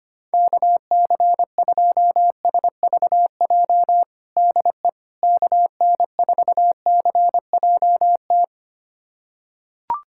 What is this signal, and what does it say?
Morse code, heard as a keyed tone, about 25 wpm, 700 Hz. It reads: KC2SVJ DE KN4CJT
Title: USA Callsign DE Callsign - 6 Characters - 25wpm